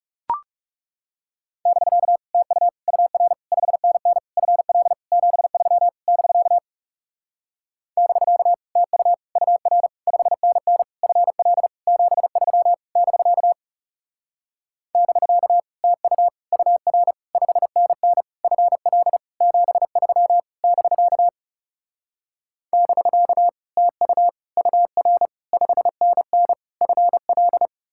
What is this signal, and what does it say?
Morse code, heard as a keyed tone, about 40 wpm, 700 Hz. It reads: <BK> TU UR 5NN FL 73 <BK> <BK> TU UR 5NN FL 73 <BK> <BK> TU UR 5NN FL 73 <BK> <BK> TU UR 5NN FL